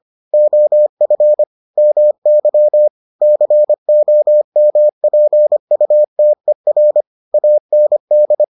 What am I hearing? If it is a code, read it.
Morse code, heard as a keyed tone, about 25 wpm, 600 Hz. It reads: OF MY COMPUTER AND